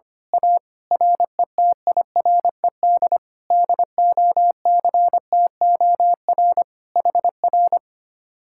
Morse code, heard as a keyed tone, about 24 wpm, 700 Hz. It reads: A RETIRED DOCTOR HR
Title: Speed-Racing Common QSO Phrases - 25wpm